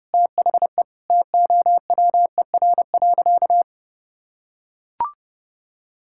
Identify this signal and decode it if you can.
Morse code, heard as a keyed tone, about 32 wpm, 700 Hz. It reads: THE TOWER.
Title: Pangram Challenge v2 - 30wpm